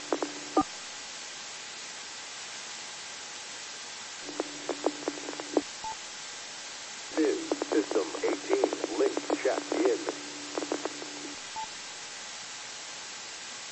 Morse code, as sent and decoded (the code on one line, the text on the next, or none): none